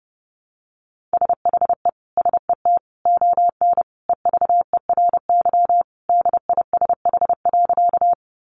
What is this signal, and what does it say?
Morse code, heard as a keyed tone, about 30 wpm, 700 Hz. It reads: SHE SET ON EVERY DISH.
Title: Sentences from Top 1300 Words 30wpm